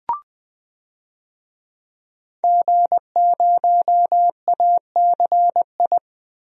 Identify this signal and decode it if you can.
Morse code, heard as a keyed tone, about 20 wpm, 700 Hz. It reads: G0ACI
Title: DXCC-100 Call Signs - 5 Characters - 20wpm